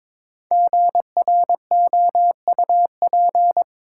Morse code, heard as a keyed tone, about 22 wpm, 700 Hz. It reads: GROUP
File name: 5 Letter Words - First 3 Letters Given - 22wpm